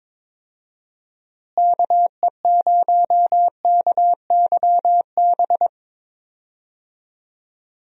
Morse code, heard as a keyed tone, about 22 wpm, 700 Hz. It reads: KE0KYB
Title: USA Call Signs - 6 Characters - Repeated 3x - 22wpm